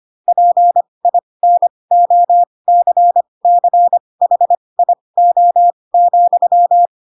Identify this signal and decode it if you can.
Morse code, heard as a keyed tone, about 25 wpm, spaced as for 20 wpm, 700 Hz. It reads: PINOCCHIO,
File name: Pinocchio / Pinocchio ch7 15 25 ik1hge